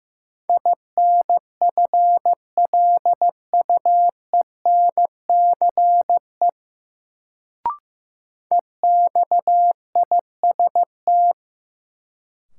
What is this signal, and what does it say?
Morse code, heard as a keyed tone, about 15 wpm, 700 Hz. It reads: INFLUENCE EXIST
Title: Sentences from Top 1100 Words - New Word Drill Alt - 15wpm